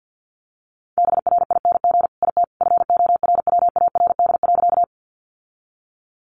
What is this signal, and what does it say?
Morse code, heard as a keyed tone, about 50 wpm, 700 Hz. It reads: BRING IT FORWARD.
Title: Sentences from Top 1300 Words 50wpm